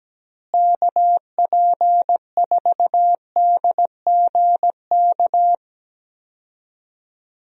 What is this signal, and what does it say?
Morse code, heard as a keyed tone, about 17 wpm, 700 Hz. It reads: KP4DGK
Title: USA Call Signs - 6 Characters - Repeated 2x - 17wpm